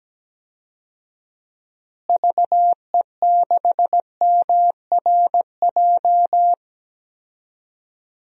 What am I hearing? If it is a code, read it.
Morse code, heard as a keyed tone, about 17 wpm, 700 Hz. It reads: VE6MRJ
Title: DXCC-100 Call Signs - 6 Characters - Repeated 2x - 17wpm